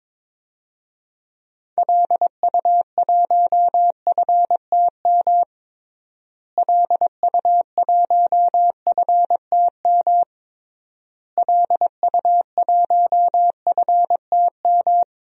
Morse code, heard as a keyed tone, about 22 wpm, 700 Hz. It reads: LU1FTM LU1FTM LU1FTM
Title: DXCC-100 Call Signs - 6 Characters - Repeated 3x - 22wpm